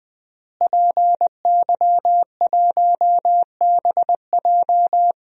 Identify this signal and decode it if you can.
Morse code, heard as a keyed tone, about 20 wpm, 700 Hz. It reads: PY1BJ
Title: DXCC-100 Call Signs - 5 Characters - Repeated 3x - 20wpm